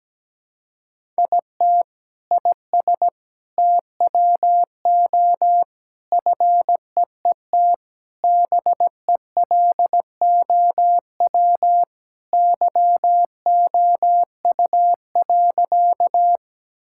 Morse code, heard as a keyed tone, about 17 wpm, 700 Hz. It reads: IT IS TWO FEET BELOW YOU.
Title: Sentences from Top 1000 Words - Review Part 5 - 17wpm